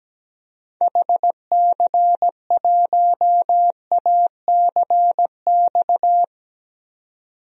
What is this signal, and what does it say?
Morse code, heard as a keyed tone, about 17 wpm, 700 Hz. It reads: HC1ACX
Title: DXCC-100 Call Signs - 6 Characters - 17wpm